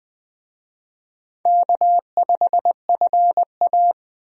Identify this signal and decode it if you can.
Morse code, heard as a keyed tone, about 20 wpm, 700 Hz. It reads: K5FA
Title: USA Call Signs - 4 Characters - Repeated 2x - 20wpm